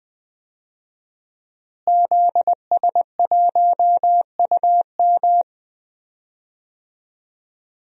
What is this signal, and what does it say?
Morse code, heard as a keyed tone, about 21 wpm, 700 Hz. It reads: ZS1UM
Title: DXCC-100 Call Signs - 5 Characters - Repeated 3x - 20wpm